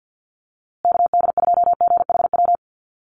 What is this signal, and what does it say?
Morse code, heard as a keyed tone, about 48 wpm, 700 Hz. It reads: KD2GHW
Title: USA Call Signs Repeated 3x 50wpm